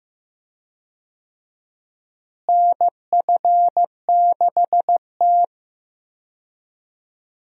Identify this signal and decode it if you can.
Morse code, heard as a keyed tone, about 15 wpm, 700 Hz. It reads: NF6T
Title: USA Call Signs - 4 Characters - Repeated 2x - 15wpm